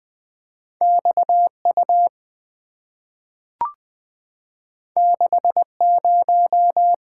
Morse code, heard as a keyed tone, about 20 wpm, 700 Hz. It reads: X U 6 0
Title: Two Letter-Number Combinations - 20wpm